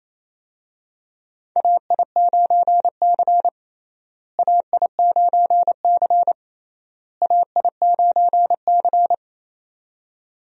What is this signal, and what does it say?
Morse code, heard as a keyed tone, about 28 wpm, 700 Hz. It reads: AI9C AI9C AI9C
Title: USA Call Signs - 4 Characters - Repeated 3x - 28wpm